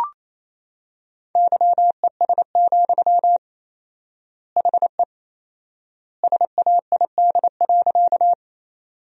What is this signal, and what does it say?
Morse code, heard as a keyed tone, about 28 wpm, 700 Hz. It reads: YES, HE SAID.
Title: 4x Word-Spacing - Sentences from Top 500 Words - 28wpm